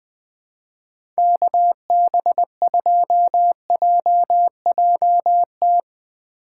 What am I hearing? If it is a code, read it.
Morse code, heard as a keyed tone, about 21 wpm, 700 Hz. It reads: KB2JJT
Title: USA Call Signs - 6 Characters - 4x Character Spacing - 20wpm